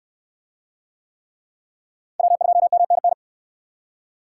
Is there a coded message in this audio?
Yes: Morse code from 2.2 to 3.1 s, heard as a keyed tone, about 67 wpm, 700 Hz. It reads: W2MNN